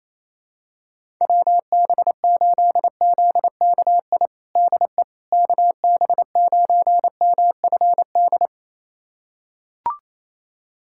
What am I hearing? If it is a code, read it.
Morse code, heard as a keyed tone, about 28 wpm, 700 Hz. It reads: WB8ZKI DE KB9MFD
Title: USA Callsign DE Callsign - 6 Characters - 28wpm